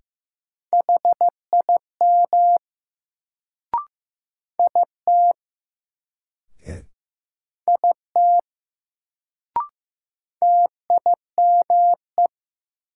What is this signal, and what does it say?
Morse code, heard as a keyed tone, about 15 wpm, 700 Hz. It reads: HIM IT IT TIME
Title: Sentences from Top 100 Words - New Word Drill - 15wpm